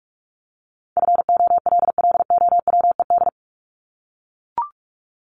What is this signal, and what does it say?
Morse code, heard as a keyed tone, about 44 wpm, 700 Hz. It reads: FOLLOWED
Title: ED Suffix - Top 100 Words with Base Words 45wpm